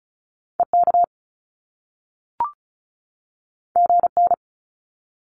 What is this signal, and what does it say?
Morse code, heard as a keyed tone, about 36 wpm, 700 Hz. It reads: EK GN
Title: Top Two Letter Combinations in Words 35wpm